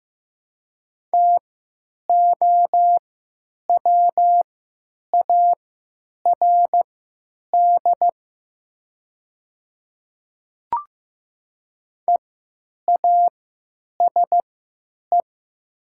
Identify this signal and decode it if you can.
Morse code, heard as a keyed tone, about 15 wpm, spaced as for 9 wpm, 700 Hz. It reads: TOWARD EASE
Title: 3x Character Spacing - Top 500 Words - 15wpm